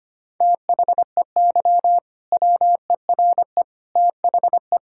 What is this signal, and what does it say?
Morse code, heard as a keyed tone, about 25 wpm, 700 Hz. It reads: THEY WERE THE
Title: Sentences from Top 1600 Words 25wpm